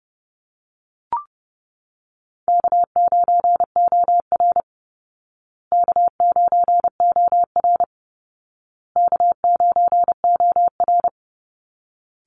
Morse code, heard as a keyed tone, about 30 wpm, 700 Hz. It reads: K9OR K9OR K9OR
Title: Easy Call Signs Repeated 3x - 30wpm